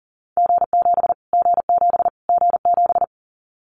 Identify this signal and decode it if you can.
Morse code, heard as a keyed tone, about 40 wpm, 700 Hz. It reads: G7 G7 G7
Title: Callsign Prefixes for Top 25 DXCC - Repeated 3x - 40wpm